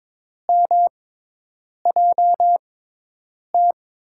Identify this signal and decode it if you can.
Morse code, heard as a keyed tone, about 22 wpm, spaced as for 8 wpm, 700 Hz. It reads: MJT
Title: USA Call Signs - 6 Characters - 6x Character Spacing - 22wpm